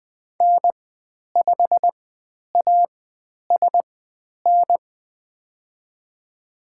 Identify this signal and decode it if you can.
Morse code, heard as a keyed tone, about 20 wpm, spaced as for 10 wpm, 700 Hz. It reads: N5ASN
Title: 095 Lesson 16g - Letter c - Call Signs 20wpm